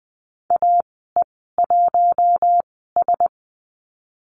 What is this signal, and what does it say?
Morse code, heard as a keyed tone, about 20 wpm, spaced as for 14 wpm, 700 Hz. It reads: AE1S
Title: 055 Lesson 10g - Letter r - Call Signs 20wpm